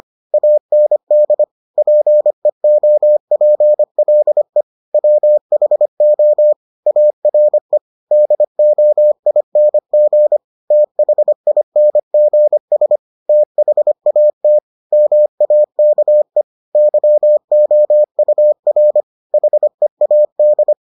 Morse code, heard as a keyed tone, about 25 wpm, 600 Hz. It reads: AND PEOPLE WHO ARE DOING THINGS THAT MAKE YOUR HEAD